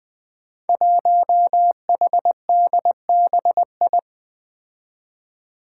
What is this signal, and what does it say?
Morse code, heard as a keyed tone, about 20 wpm, 700 Hz. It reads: 1HDBI
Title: Five Challenging Letter-Number Combinations 20wpm